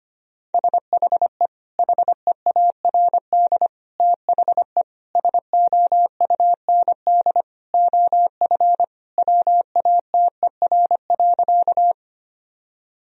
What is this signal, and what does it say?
Morse code, heard as a keyed tone, about 25 wpm, 700 Hz. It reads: SHE HEARD THE SOUND OF WATER.
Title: Sentences from Top 1000 Words - Review Part 1 - 25wpm